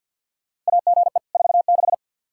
Morse code, heard as a keyed tone, about 47 wpm, 700 Hz. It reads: AGE 46
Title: K1USN SST QSO Practice 50wpm